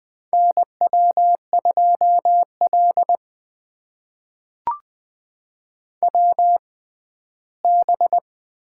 Morse code, heard as a keyed tone, about 20 wpm, spaced as for 14 wpm, 700 Hz. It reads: NW2L WB
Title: USA Call Signs - 6x Character Spacing - 20wpm